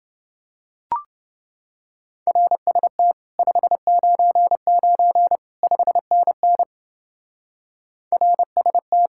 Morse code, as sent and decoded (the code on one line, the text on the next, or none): .-. ... - / ..... ----. ----. / ..... -. -. / .-. ... -
RST 599 5NN RST